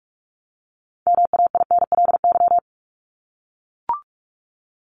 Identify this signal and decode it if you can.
Morse code, heard as a keyed tone, about 44 wpm, 700 Hz. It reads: MAINLY
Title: Top 1251-1500 Words 45wpm